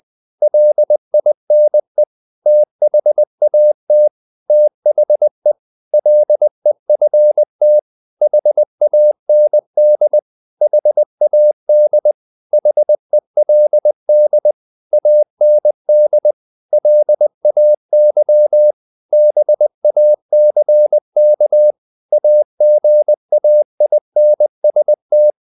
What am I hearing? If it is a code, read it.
Morse code, heard as a keyed tone, about 20 wpm, 600 Hz. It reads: LINE THAT THE LEFT HAND HAD HELD AND LAY BACK AGAINST
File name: the old man and the sea / the old man and the sea part 1 20wpm